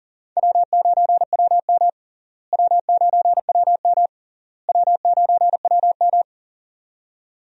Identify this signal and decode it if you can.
Morse code, heard as a keyed tone, about 40 wpm, 700 Hz. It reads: W9WM W9WM W9WM
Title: USA Call Signs - 4 Characters - Repeated 3x - 40wpm